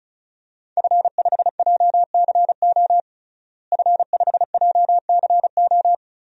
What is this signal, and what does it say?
Morse code, heard as a keyed tone, about 34 wpm, 700 Hz. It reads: F5JCO F5JCO
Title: DXCC-100 Call Signs - 5 Characters - Repeated 3x - 35wpm